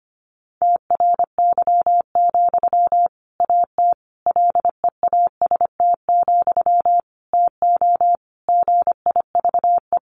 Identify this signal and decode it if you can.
Morse code, heard as a keyed tone, about 25 wpm, 700 Hz. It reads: TRY, AT LEAST, TO GIVE
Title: Sentences from Top 700 Words 25wpm